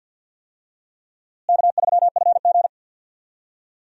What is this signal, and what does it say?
Morse code, heard as a keyed tone, about 52 wpm, 700 Hz. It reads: K3FG
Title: USA Call Signs - 4 Characters - Repeated 2x - 50wpm